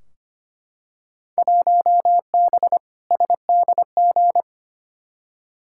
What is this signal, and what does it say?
Morse code, heard as a keyed tone, about 25 wpm, 700 Hz. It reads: 1B SDG
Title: Field Day Exchanges Encore - 25wpm